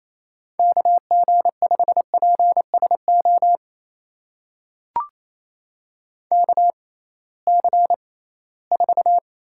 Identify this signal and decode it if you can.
Morse code, heard as a keyed tone, about 28 wpm, spaced as for 18 wpm, 700 Hz. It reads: KG5PSO KC4
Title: USA Call Signs - 6 Characters - 6x Character Spacing - 28wpm